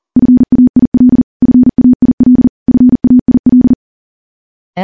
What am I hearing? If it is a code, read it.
Morse code, heard as a keyed tone, about 40 wpm, 250 Hz. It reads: FAIL FAIL FAIL
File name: morse / morse-training-40WPM